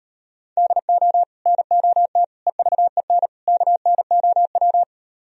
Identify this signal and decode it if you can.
Morse code, heard as a keyed tone, about 38 wpm, 700 Hz. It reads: DO NOT EVEN KNOW